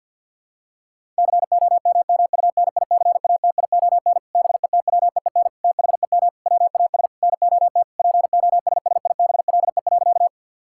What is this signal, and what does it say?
Morse code, heard as a keyed tone, about 50 wpm, 700 Hz. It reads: COMMUNICATION BETWEEN THEM WAS NOT POSSIBLE.
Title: Sentences from Top 1000 Words - Review Part 6 - 50wpm